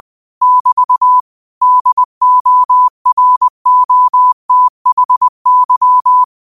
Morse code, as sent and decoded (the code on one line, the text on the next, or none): -...- / -.. --- .-. --- - .... -.--
<BT> DOROTHY